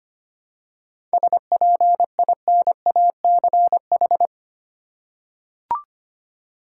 Morse code, heard as a keyed tone, about 25 wpm, 700 Hz. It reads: SPINACH
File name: practice / 25wpm / Difficult Words with Hints - 25wpm